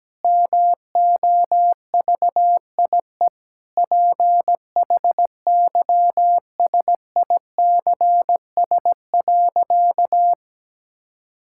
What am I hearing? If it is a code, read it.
Morse code, heard as a keyed tone, about 17 wpm, 700 Hz. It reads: MOVIE PHYSICS.
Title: Sentences from Top 1000 Words - Review Part 5 - 17wpm